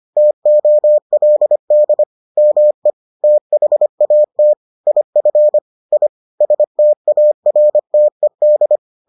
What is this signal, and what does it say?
Morse code, heard as a keyed tone, about 25 wpm, 600 Hz. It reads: TOLD ME THAT IF I STARTED